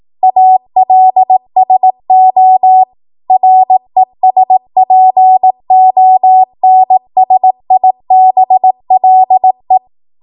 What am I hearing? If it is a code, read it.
Morse code, heard as a keyed tone, about 18 wpm, 750 Hz. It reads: ALSO RESPONSIBLE
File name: Morse / 18WPM